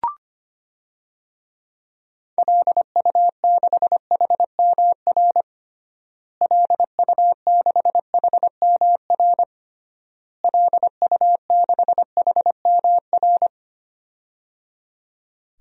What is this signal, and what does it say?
Morse code, heard as a keyed tone, about 25 wpm, 700 Hz. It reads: LU6HMR LU6HMR LU6HMR
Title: DXCC-100 Call Signs - 6 Characters - Repeated 3x - 25wpm